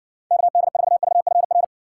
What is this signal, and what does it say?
Morse code, heard as a keyed tone, about 60 wpm, 700 Hz. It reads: KD4VFR